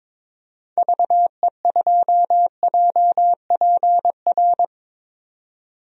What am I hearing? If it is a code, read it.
Morse code, heard as a keyed tone, about 22 wpm, 700 Hz. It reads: VE2JPR
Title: DXCC-100 Call Signs - 6 Characters - 22wpm